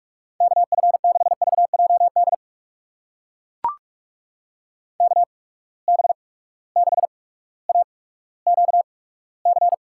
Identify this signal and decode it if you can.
Morse code, heard as a keyed tone, about 42 wpm, spaced as for 19 wpm, 700 Hz. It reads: KF6VJD KB6AQC
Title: USA Call Signs - 6 Characters - 8x Character Spacing - 45wpm